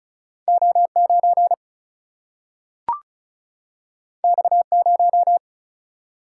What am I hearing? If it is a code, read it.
Morse code, heard as a keyed tone, about 36 wpm, 700 Hz. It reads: O9 X0
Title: Single-Letter Single-Number 35wpm